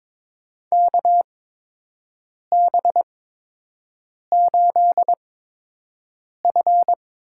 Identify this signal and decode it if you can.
Morse code, heard as a keyed tone, about 22 wpm, spaced as for 6 wpm, 700 Hz. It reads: KB8F
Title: USA Call Signs - 6 Characters - 8x Character Spacing - 22wpm